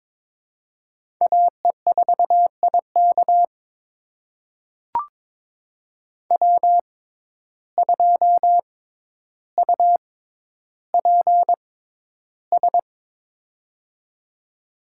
Morse code, heard as a keyed tone, about 22 wpm, spaced as for 11 wpm, 700 Hz. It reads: AE4IK W2UPS
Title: USA Call Signs - 6x Character Spacing - 22wpm